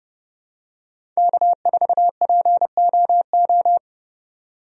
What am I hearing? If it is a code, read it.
Morse code, heard as a keyed tone, about 30 wpm, 700 Hz. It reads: K4POO
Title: USA Call Signs - 5 Characters - 30wpm